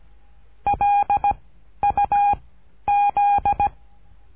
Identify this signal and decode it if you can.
Morse code, heard as a keyed tone, about 17 wpm, 810 Hz. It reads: L U Z
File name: cursos / CURSO DE TELEGRAFIA / Telegrafia - Trilha 13